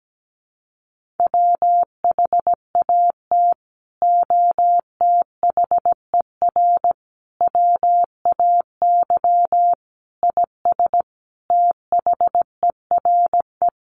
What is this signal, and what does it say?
Morse code, heard as a keyed tone, about 17 wpm, 700 Hz. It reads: WHAT OTHER WAY IS THERE
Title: Sentences from Top 100 Words 17wpm